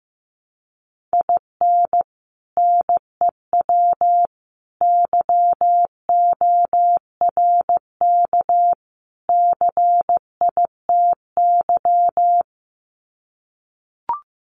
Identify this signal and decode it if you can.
Morse code, heard as a keyed tone, about 15 wpm, 700 Hz. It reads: IN NEW YORK CITY
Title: Sets of 4 Words 15wpm